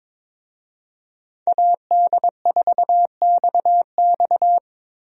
Morse code, heard as a keyed tone, about 22 wpm, 700 Hz. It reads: AD4XX
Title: USA Call Signs - 5 Characters - Repeated 3x - 22wpm